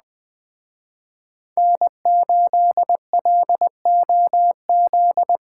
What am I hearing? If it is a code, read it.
Morse code, heard as a keyed tone, about 20 wpm, 700 Hz. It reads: N8LOZ
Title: Speed Racing - USA 1x3 Call Signs 17wpm